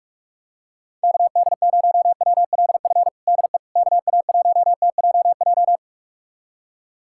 Morse code, heard as a keyed tone, about 46 wpm, 700 Hz. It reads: KD0WLF DE KA1TJJ
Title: USA Callsign DE Callsign - 6 Characters - 45wpm